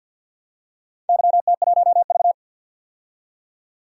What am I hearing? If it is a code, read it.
Morse code, heard as a keyed tone, about 53 wpm, 700 Hz. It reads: YT1V
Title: DXCC-100 Call Signs - 4 Characters - 50wpm